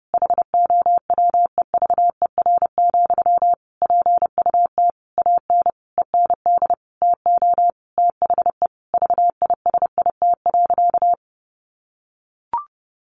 Morse code, heard as a keyed tone, about 30 wpm, 700 Hz. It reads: HOWEVER, PUT AN END TO THE VISIT.
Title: Sentences from Top 1200 Words 30wpm